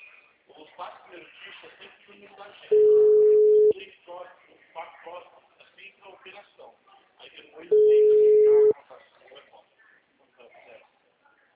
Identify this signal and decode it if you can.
Ringback tone (Europe, 425 Hz), with speech.